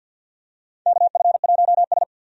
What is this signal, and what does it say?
Morse code, heard as a keyed tone, about 52 wpm, 700 Hz. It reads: KF1S
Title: USA Call Signs - 4 Characters - 50wpm